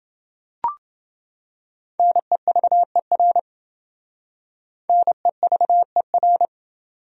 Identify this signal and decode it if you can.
Morse code, heard as a keyed tone, about 27 wpm, 700 Hz. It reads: NEVER NEVER